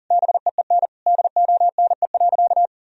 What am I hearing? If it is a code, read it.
Morse code, heard as a keyed tone, about 40 wpm, 700 Hz. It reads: BEEN DONE.
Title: Sentences from Top 1000 Words - Review Part 1 - 40wpm